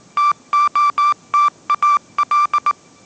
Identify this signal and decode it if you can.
Morse code, heard as a keyed tone, about 19 wpm, 1170 Hz. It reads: TOTAL
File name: mrx083 20wpm